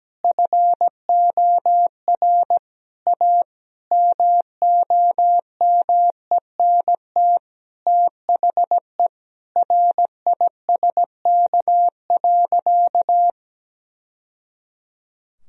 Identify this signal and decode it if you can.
Morse code, heard as a keyed tone, about 17 wpm, 700 Hz. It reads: FOR A MOMENT THE RISK.
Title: Sentences from Top 1000 Words - Review Part 4 - 17wpm